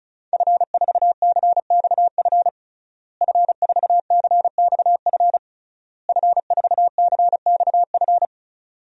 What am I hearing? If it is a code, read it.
Morse code, heard as a keyed tone, about 34 wpm, 700 Hz. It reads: F4CXF F4CXF F4CXF
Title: DXCC-100 Call Signs - 5 Characters - Repeated 3x - 35wpm